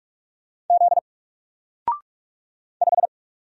Morse code, heard as a keyed tone, about 45 wpm, 700 Hz. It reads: Z 5
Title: Single Letter-Number Alt 45wpm